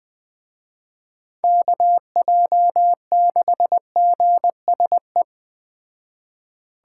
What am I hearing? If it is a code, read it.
Morse code, heard as a keyed tone, about 20 wpm, 700 Hz. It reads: KJ6GSE